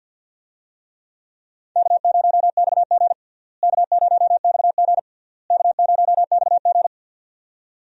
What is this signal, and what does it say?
Morse code, heard as a keyed tone, about 49 wpm, 700 Hz. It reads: K0XG K0XG K0XG
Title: USA Call Signs - 4 Characters - Repeated 3x - 50wpm